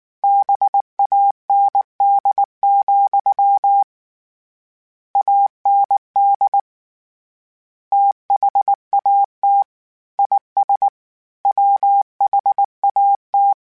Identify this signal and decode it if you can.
Morse code, heard as a keyed tone, about 19 wpm, 800 Hz. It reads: BAND, AND THAT IS WHAT
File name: Misc / Misc-mp3s / BIG @18